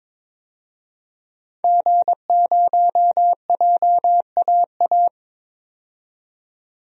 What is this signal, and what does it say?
Morse code, heard as a keyed tone, about 22 wpm, 700 Hz. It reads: G0JAA
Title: DXCC-100 Call Signs - 5 Characters - 22wpm